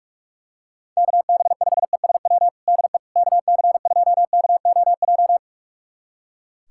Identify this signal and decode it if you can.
Morse code, heard as a keyed tone, about 45 wpm, 700 Hz. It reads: KB5ESW DE KC2KOJ
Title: USA Callsign DE Callsign 45wpm